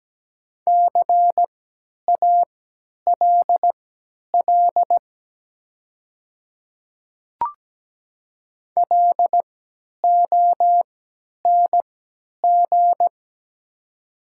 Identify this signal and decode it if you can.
Morse code, heard as a keyed tone, about 17 wpm, spaced as for 10 wpm, 700 Hz. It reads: CALL LONG
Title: 3x Character Spacing - Top 100 Words - 17wpm